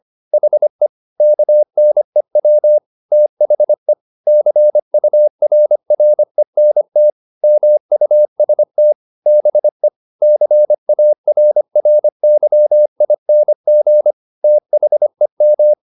Morse code, heard as a keyed tone, about 25 wpm, 600 Hz. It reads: HE KNEW THE CURRENT MUST BE CARRYING THEM
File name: the old man and the sea part 1 25wpm